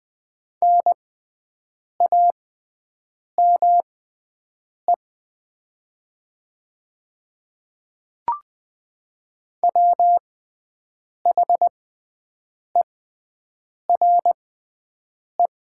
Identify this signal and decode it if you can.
Morse code, heard as a keyed tone, about 20 wpm, spaced as for 7 wpm, 700 Hz. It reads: NAME WHERE
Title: 6x Character Spacing - Top 200 Words - 20wpm